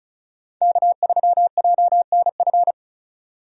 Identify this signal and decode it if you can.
Morse code, heard as a keyed tone, about 36 wpm, 700 Hz. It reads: K3JNF
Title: USA Call Signs - 5 Characters - 35wpm